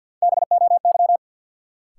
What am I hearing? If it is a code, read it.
Morse code, heard as a keyed tone, about 53 wpm, 700 Hz. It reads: BOY
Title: Sentences from Top 200 Words - New Word Drill - 50wpm